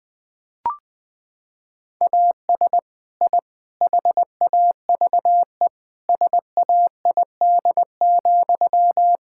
Morse code, heard as a keyed tone, about 20 wpm, 700 Hz. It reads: AS I HAVE SAID,
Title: Sentences from Top 1000 Words - Review Part 6 - 20wpm